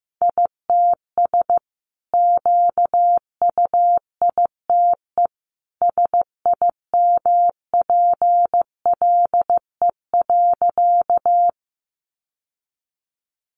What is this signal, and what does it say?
Morse code, heard as a keyed tone, about 15 wpm, 700 Hz. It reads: ITS QUITE SIMPLE.
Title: Sentences from Top 700 Words 15wpm